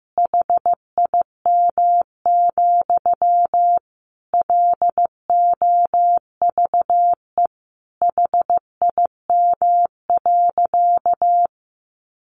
Morse code, heard as a keyed tone, about 15 wpm, 700 Hz. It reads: HIM, LOVE HIM.
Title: Sentences from Top 1100 Words 15wpm